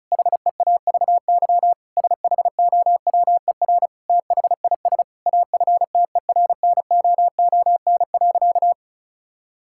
Morse code, heard as a keyed tone, about 36 wpm, 700 Hz. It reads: HEAVY SHOWER THIS AFTERNOON.
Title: Sentences from Top 1600 Words 35wpm